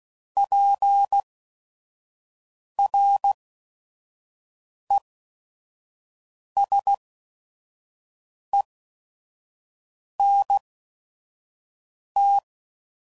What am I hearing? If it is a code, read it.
Morse code, heard as a keyed tone, about 16 wpm, spaced as for 5 wpm, 800 Hz. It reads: PRESENT
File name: CW Code Groups @5WPM / Technical Subjects / Gates Intro